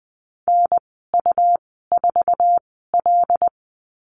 Morse code, heard as a keyed tone, about 20 wpm, spaced as for 15 wpm, 700 Hz. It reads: NU4L
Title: 167 Lesson 27g - Number 7 - Call Signs 20wpm